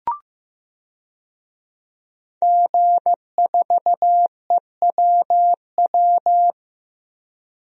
Morse code, heard as a keyed tone, about 15 wpm, 700 Hz. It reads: G4EWW
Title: DXCC-100 Call Signs - 5 Characters - Repeated 2x - 15wpm